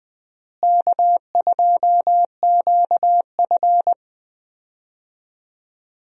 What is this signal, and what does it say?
Morse code, heard as a keyed tone, about 20 wpm, 700 Hz. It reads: K2QF
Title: USA Call Signs - 4 Characters - Repeated 2x - 20wpm